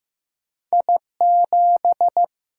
Morse code, heard as a keyed tone, about 15 wpm, 700 Hz. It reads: I 7
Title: Single-Letter Single-Number 15wpm